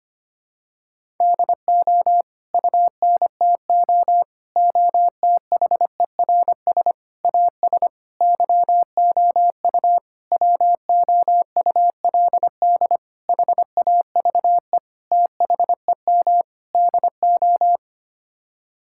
Morse code, heard as a keyed tone, about 25 wpm, 700 Hz. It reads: DO UNTO OTHERS AS YOU WOULD HAVE THEM DO
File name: Sets of 9 Words - 25wpm